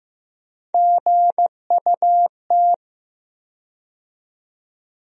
Morse code, heard as a keyed tone, about 15 wpm, 700 Hz. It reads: GUT
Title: USA Call Sign Postfixes - 3 Letters 15wpm